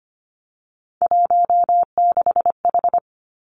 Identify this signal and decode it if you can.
Morse code, heard as a keyed tone, about 25 wpm, 700 Hz. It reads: 16H
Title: Three Challenging Letter-Number Combinations 25wpm